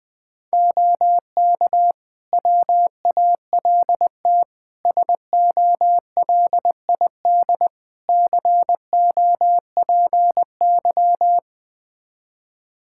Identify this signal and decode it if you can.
Morse code, heard as a keyed tone, about 21 wpm, 700 Hz. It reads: OK WALT SOLID COPY
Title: Practice QSO - Exchange 3 - More Info - 20wpm